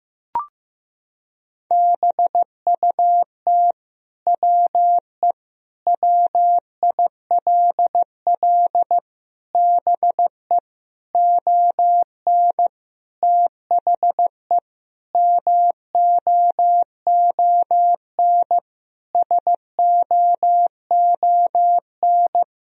Morse code, heard as a keyed tone, about 16 wpm, 700 Hz. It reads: BUT WE WILL BE ON THE MOON SOON
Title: Sentences from Top 1000 Words - Review Part 3 - 15wpm